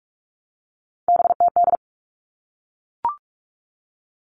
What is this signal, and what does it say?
Morse code, heard as a keyed tone, about 44 wpm, 700 Hz. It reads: BTD
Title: All Cut Numbers - 3-Digit 45wpm